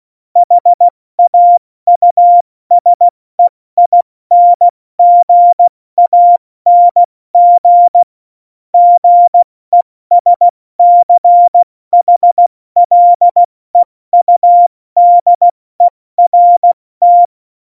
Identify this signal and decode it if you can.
Morse code, heard as a keyed tone, about 16 wpm, spaced as for 14 wpm, 700 Hz. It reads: HAUSEINGANG GESCHLEUDERT
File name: Morsekurs / Witze / Witz 05. Januar 2021, 70 F80